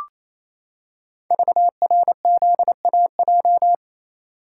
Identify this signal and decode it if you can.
Morse code, heard as a keyed tone, about 27 wpm, 700 Hz. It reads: VRZAJ
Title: Code Groups - 5 Letters - 28wpm